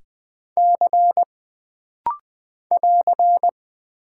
Morse code, heard as a keyed tone, about 20 wpm, 700 Hz. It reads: C <AR>
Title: Single Character (Letter:Number:Punctuation:Prosign) - Rapid-Fire - 20wpm